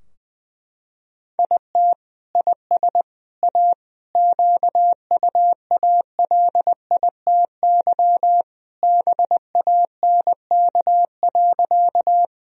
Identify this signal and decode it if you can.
Morse code, heard as a keyed tone, about 20 wpm, 700 Hz. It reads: IT IS A QUALITY BANK.
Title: Sentences from Top 1000 Words - Review Part 4 - 20wpm